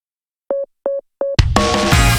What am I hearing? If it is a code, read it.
There is no code.